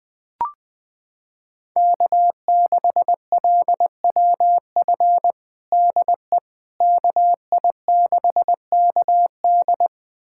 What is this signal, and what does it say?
Morse code, heard as a keyed tone, about 20 wpm, 700 Hz. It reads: K6LWF DE KI6KD
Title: USA Callsign DE Callsign - 5 Characters - 20wpm